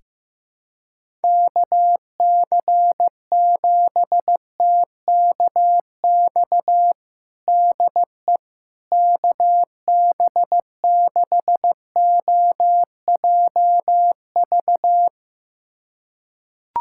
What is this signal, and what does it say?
Morse code, heard as a keyed tone, about 15 wpm, 700 Hz. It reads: KC7TKX DE KB6OJV
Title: USA Callsign DE Callsign - 6 Characters - 15wpm